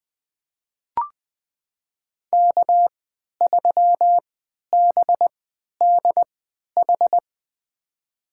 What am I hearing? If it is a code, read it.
Morse code, heard as a keyed tone, about 20 wpm, spaced as for 11 wpm, 700 Hz. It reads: K3BDH
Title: USA Call Signs - 3x Character Spacing - 20wpm